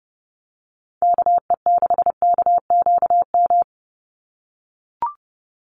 Morse code, heard as a keyed tone, about 30 wpm, 700 Hz. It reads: KE6KQM
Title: USA Call Signs 30wpm